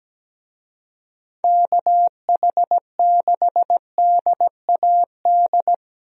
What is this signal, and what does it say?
Morse code, heard as a keyed tone, about 17 wpm, 700 Hz. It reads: KH6DAD
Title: DXCC-100 Call Signs - 6 Characters - 17wpm